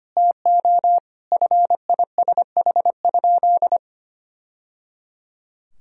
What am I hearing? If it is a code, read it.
Morse code, heard as a keyed tone, about 25 wpm, 700 Hz. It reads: TO FISH?
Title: Sentences from Top 1000 Words - Review Part 1 - 25wpm